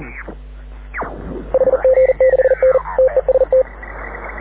Morse code, heard as a keyed tone, about 40 wpm, 520 Hz. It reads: HG6N TEST
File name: AUDIO2019 / HG6N CW SSB MIX